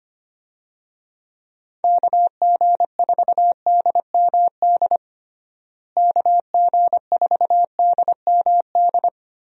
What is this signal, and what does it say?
Morse code, heard as a keyed tone, about 25 wpm, 700 Hz. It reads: KG4DMD KG4DMD
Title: USA Call Signs - 6 Characters - Repeated 3x - 25wpm